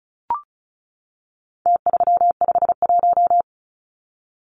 Morse code, heard as a keyed tone, about 34 wpm, 700 Hz. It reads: T351
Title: Numbers - Contest Style - 4-Digit 35wpm